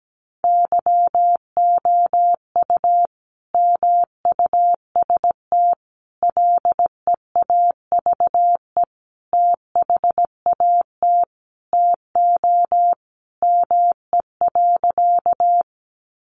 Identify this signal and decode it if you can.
Morse code, heard as a keyed tone, about 17 wpm, 700 Hz. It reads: YOU MUST LEAVE THAT TO ME.
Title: Sentences from Top 400 Words 17wpm